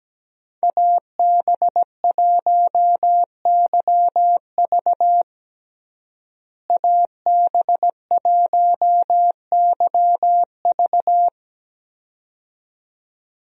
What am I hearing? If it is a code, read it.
Morse code, heard as a keyed tone, about 17 wpm, 700 Hz. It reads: AB1YV AB1YV
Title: USA Call Signs - 5 Characters - Repeated 2x - 17wpm